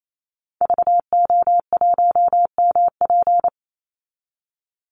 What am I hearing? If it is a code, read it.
Morse code, heard as a keyed tone, about 28 wpm, 700 Hz. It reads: VO1MP
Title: Call Signs Repeated 2x - 28wpm